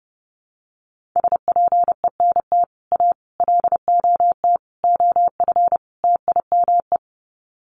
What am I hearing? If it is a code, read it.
Morse code, heard as a keyed tone, about 30 wpm, 700 Hz. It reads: SPENT A LOT OF TIME
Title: Sets of 5 Words 30wpm